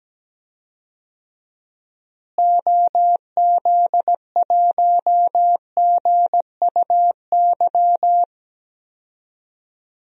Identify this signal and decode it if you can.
Morse code, heard as a keyed tone, about 17 wpm, 700 Hz. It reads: OZ1GUY
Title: DXCC-100 Call Signs - 6 Characters - Repeated 3x - 17wpm